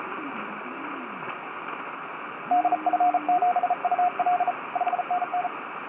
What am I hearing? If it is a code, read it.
Morse code, heard as a keyed tone, about 38 wpm, 720 Hz, with radio noise. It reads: DF7UL 5NN
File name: 6o0cw-on-10m-cw